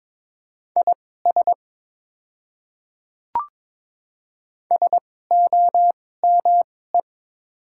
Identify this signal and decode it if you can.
Morse code, heard as a keyed tone, about 22 wpm, spaced as for 16 wpm, 700 Hz. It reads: IS SOME
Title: 2x Character Spacing - Top 100 Words - 22wpm